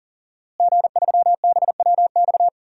Morse code, heard as a keyed tone, about 37 wpm, 700 Hz. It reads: G3BWX
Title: DXCC-100 Call Signs - 5 Characters - Repeated 3x - 40wpm